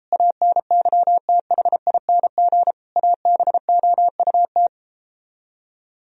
Morse code, heard as a keyed tone, about 37 wpm, 700 Hz. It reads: ANYTHING ABOUT